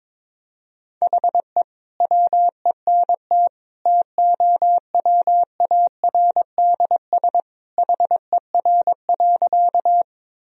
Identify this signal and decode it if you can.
Morse code, heard as a keyed tone, about 22 wpm, 700 Hz. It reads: HE WENT TOWARDS HER.
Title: Sentences from Top 1000 Words - Review Part 2 - 22wpm